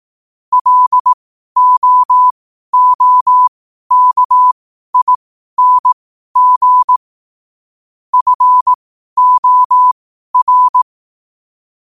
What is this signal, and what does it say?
Morse code, heard as a keyed tone, about 18 wpm, spaced as for 12 wpm, 1000 Hz. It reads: LOOKING FOR
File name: qotdfiles / 12wpm